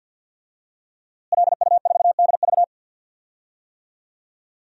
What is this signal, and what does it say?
Morse code, heard as a keyed tone, about 51 wpm, 700 Hz. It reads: LU4DV
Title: DXCC-100 Call Signs - 5 Characters - Repeated 2x - 50wpm